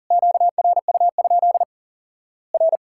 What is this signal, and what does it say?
Morse code, heard as a keyed tone, about 40 wpm, 700 Hz. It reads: QRU? R
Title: Worked All Europe DX Contest - QTC Exchange - 40wpm